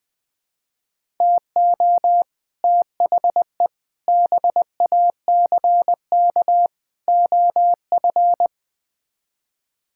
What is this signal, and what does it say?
Morse code, heard as a keyed tone, about 20 wpm, 700 Hz. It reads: TO THE BACK OF
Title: Build A Phrase - 5-Word Phrases - 20wpm